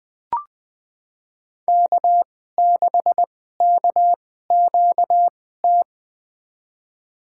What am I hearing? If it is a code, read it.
Morse code, heard as a keyed tone, about 20 wpm, spaced as for 14 wpm, 700 Hz. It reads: K6KQT
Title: USA Call Signs - 5 Characters - 2x Character Spacing - 20wpm